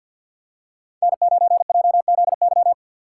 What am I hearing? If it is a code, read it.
Morse code, heard as a keyed tone, about 53 wpm, 700 Hz. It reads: N9JZY
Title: K1USN SST QSO Practice - 50wpm